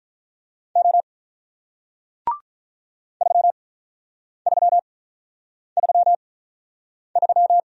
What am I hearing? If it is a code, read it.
Morse code, heard as a keyed tone, about 47 wpm, spaced as for 9 wpm, 700 Hz. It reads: O 3333